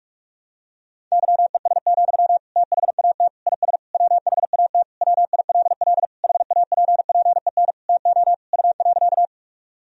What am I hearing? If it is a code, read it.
Morse code, heard as a keyed tone, about 46 wpm, 700 Hz. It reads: YES, THAT IS WHAT WILL HAPPEN TO U.
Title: Sentences from Top 500 Words and Abbreviations 45wpm